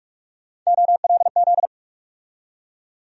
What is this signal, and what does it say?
Morse code, heard as a keyed tone, about 47 wpm, 700 Hz. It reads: OLZ
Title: USA Call Sign Postfixes - 3 Letters 45wpm